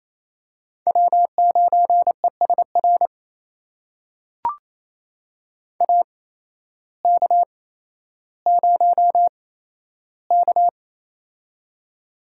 Morse code, heard as a keyed tone, about 28 wpm, spaced as for 13 wpm, 700 Hz. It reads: W9ESR AK0K
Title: USA Call Signs - 8x Character Spacing - 28wpm